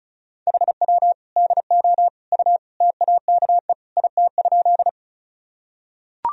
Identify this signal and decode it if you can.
Morse code, heard as a keyed tone, about 36 wpm, 700 Hz. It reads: HW DO U TAKE IT?
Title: Sentences from Top 200 Words and Abbreviations 35wpm